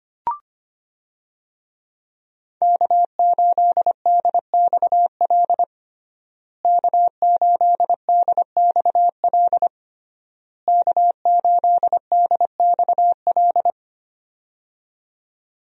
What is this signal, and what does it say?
Morse code, heard as a keyed tone, about 25 wpm, 700 Hz. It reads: K8DXL K8DXL K8DXL
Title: USA Call Signs - 5 Characters - Repeated 3x - 25wpm